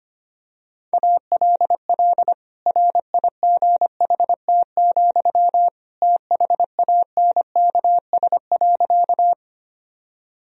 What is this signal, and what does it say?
Morse code, heard as a keyed tone, about 25 wpm, 700 Hz. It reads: ALL RIGHT, THANKS.
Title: Sentences from Top 1000 Words - Review Part 3 - 25wpm